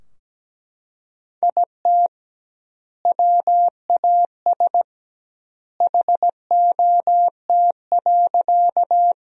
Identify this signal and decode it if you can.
Morse code, heard as a keyed tone, about 17 wpm, 700 Hz. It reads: IT WAS HOT.
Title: 2x Word-Spacing - Sentences from Top 500 Words - 17wpm